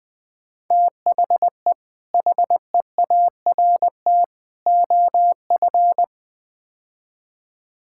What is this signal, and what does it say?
Morse code, heard as a keyed tone, about 20 wpm, 700 Hz. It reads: THE HEART OF
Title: Build A Phrase - 4-Word Phrases - 20wpm